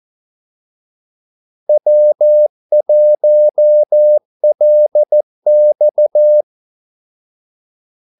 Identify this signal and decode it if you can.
Morse code, heard as a keyed tone, about 14 wpm, 600 Hz. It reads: W1LX
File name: Callsigns-14wpm